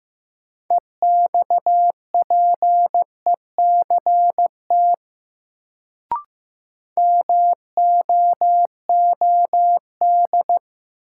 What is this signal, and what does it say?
Morse code, heard as a keyed tone, about 15 wpm, 700 Hz. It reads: EXPECT MOOD
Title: Sentences from Top 1100 Words - New Word Drill Alt - 15wpm